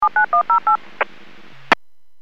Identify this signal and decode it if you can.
DTMF, heard as telephone touch tones, tones of about 110 ms. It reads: 79108
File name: WPS / 00156 13-42 WPS 451.25 WPS